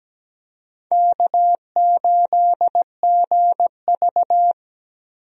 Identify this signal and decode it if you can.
Morse code, heard as a keyed tone, about 17 wpm, 700 Hz. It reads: K8GV
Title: USA Call Signs - 4 Characters - Repeated 2x - 17wpm